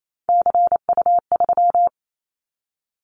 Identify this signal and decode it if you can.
Morse code, heard as a keyed tone, about 28 wpm, 700 Hz. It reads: CU3
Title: Callsign Prefixes for Top 100 DXCC - 28wpm